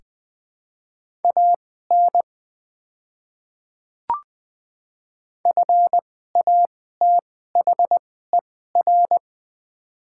Morse code, heard as a keyed tone, about 20 wpm, spaced as for 14 wpm, 700 Hz. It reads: AN FATHER
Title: 2x Character Spacing - Top 200 Words - 20wpm